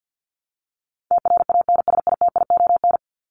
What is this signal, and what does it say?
Morse code, heard as a keyed tone, about 50 wpm, 700 Hz. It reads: TRANSITION
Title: Top 1751-2000 Words 50wpm